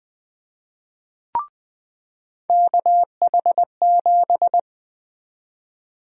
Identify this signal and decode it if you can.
Morse code, heard as a keyed tone, about 20 wpm, 700 Hz. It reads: KH7
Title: USA Call Sign Prefixes - 20wpm